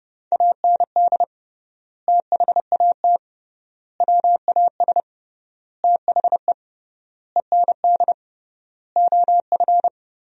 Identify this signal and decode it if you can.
Morse code, heard as a keyed tone, about 30 wpm, 700 Hz. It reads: AND THAT WAS THE END OF
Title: 3x Word-Spacing - Sentences from Top 200 Words - 30wpm